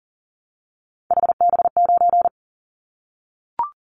Morse code, heard as a keyed tone, about 40 wpm, 700 Hz. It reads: HB9
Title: Callsign Prefixes for Top 25 DXCC - Repeated 3x - 40wpm